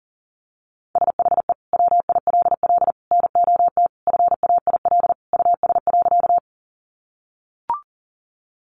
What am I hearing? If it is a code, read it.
Morse code, heard as a keyed tone, about 40 wpm, 700 Hz. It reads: SHE WILL NOT FAIL US.
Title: Sentences from Top 1100 Words 40wpm